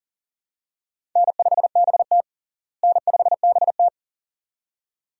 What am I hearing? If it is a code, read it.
Morse code, heard as a keyed tone, about 38 wpm, 700 Hz. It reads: N5BT N5BT
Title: USA Call Signs - 4 Characters - Repeated 2x - 40wpm